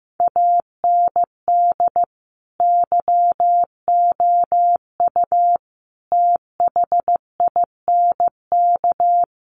Morse code, heard as a keyed tone, about 15 wpm, 700 Hz. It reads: AND YOU THINK
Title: Sentences from Top 1500 Words 15wpm